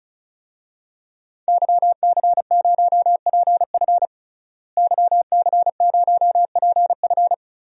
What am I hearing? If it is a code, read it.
Morse code, heard as a keyed tone, about 36 wpm, 700 Hz. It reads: YC0PF YC0PF
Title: DXCC-100 Call Signs - 5 Characters - Repeated 2x - 35wpm